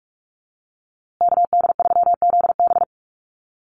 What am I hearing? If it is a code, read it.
Morse code, heard as a keyed tone, about 44 wpm, 700 Hz. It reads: KD3ZB
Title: USA Call Signs Repeated 2x 45wpm